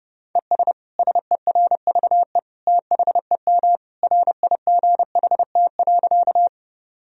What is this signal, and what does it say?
Morse code, heard as a keyed tone, about 32 wpm, 700 Hz. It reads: ES SERVE THEM RIGHT.
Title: Sentences from Top 400 Words and Abbreviations 30wpm